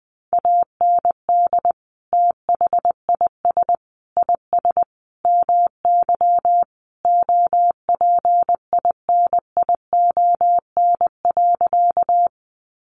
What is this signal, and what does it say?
Morse code, heard as a keyed tone, about 20 wpm, 700 Hz. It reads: AND THIS IS MY OPINION.
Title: Sentences from Top 1200 Words 20wpm